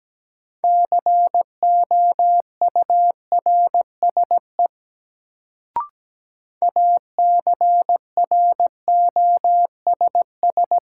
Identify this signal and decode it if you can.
Morse code, heard as a keyed tone, about 17 wpm, 700 Hz. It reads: COURSE ACROSS
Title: Sets of 2 Words - New Word Drill - 17wpm